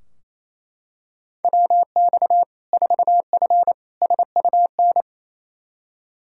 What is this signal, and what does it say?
Morse code, heard as a keyed tone, about 28 wpm, 700 Hz. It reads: WX 4F SUN
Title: Practice QSO - Exchange 2 - Info - 28wpm